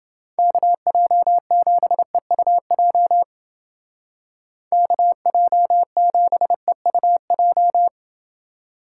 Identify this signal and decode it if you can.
Morse code, heard as a keyed tone, about 29 wpm, 700 Hz. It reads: KJ7EUJ KJ7EUJ